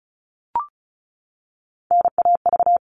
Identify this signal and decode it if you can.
Morse code, heard as a keyed tone, about 34 wpm, 700 Hz. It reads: NAV
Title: All Cut Numbers - 3-Digit 35wpm